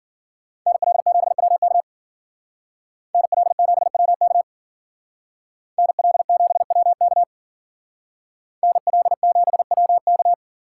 Morse code, heard as a keyed tone, about 52 wpm, 700 Hz. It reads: NL7WK NL7WK NL7WK NL7WK